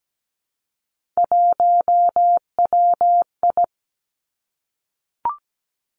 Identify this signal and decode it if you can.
Morse code, heard as a keyed tone, about 17 wpm, 700 Hz. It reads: 1WI
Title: Three Challenging Letter-Number Combinations 17wpm